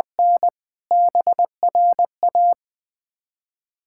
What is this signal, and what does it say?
Morse code, heard as a keyed tone, about 21 wpm, 700 Hz. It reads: N BRA
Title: Ontario QSO Party Exchange 20wpm